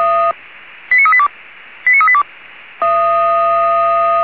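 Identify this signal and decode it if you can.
Five-tone selective call (ZVEI), heard as quick runs of five tones, tones of about 70 ms. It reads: 89382 89382
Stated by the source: ZVEI five-tone selective call received by SDR.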